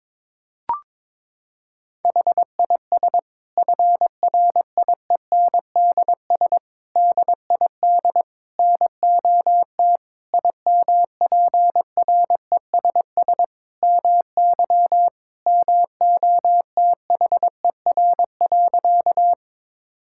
Morse code, heard as a keyed tone, about 22 wpm, 700 Hz. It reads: HIS FRIENDS DID NOT IMPRESS MY MOTHER.
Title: Sentences from Top 1600 Words 22wpm